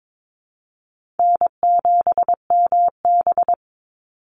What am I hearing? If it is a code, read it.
Morse code, heard as a keyed tone, about 22 wpm, 700 Hz. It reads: N7MB
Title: Easy Call Signs Repeated 2x - 22wpm